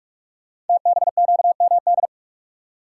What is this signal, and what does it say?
Morse code, heard as a keyed tone, about 44 wpm, 700 Hz. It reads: TBQMD
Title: Code Groups - 5 Letters - 45wpm